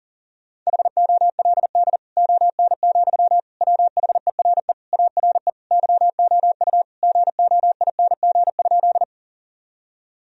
Morse code, heard as a keyed tone, about 37 wpm, 700 Hz. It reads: HOLD ON, WHERE ARE YOU GOING?
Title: Sentences from Top 1000 Words - Review Part 2 - 40wpm